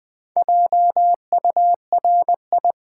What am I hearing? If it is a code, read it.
Morse code, heard as a keyed tone, about 20 wpm, 700 Hz. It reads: JURI
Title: Build A Word - 4 Letters At A Time - 12-Letter Words - 20wpm